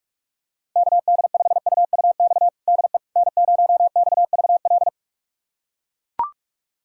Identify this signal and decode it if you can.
Morse code, heard as a keyed tone, about 41 wpm, 700 Hz. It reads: KD5UUX DE N0XVL
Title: USA Callsign DE Callsign 45wpm